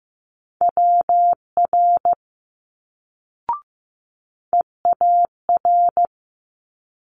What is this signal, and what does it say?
Morse code, heard as a keyed tone, about 15 wpm, 700 Hz. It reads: WR EAR
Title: Multi-letter Phonemes Quick Fire 15wpm